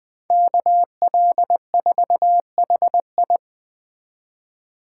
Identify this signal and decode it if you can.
Morse code, heard as a keyed tone, about 20 wpm, 700 Hz. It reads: KL4HI
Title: USA Call Signs - 5 Characters - 3x Character Spacing - 20wpm